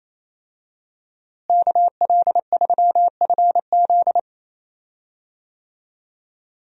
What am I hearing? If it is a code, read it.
Morse code, heard as a keyed tone, about 28 wpm, 700 Hz. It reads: KL3FZ
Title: USA Call Signs - 5 Characters - Repeated 3x - 28wpm